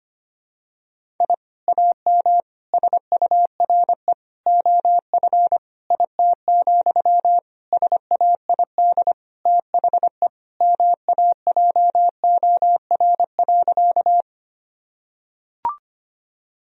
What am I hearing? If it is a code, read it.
Morse code, heard as a keyed tone, about 25 wpm, 700 Hz. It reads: I AM SURE OF IT, SAID THE MAJOR.
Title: Sentences from Top 1000 Words - Review Part 4 - 25wpm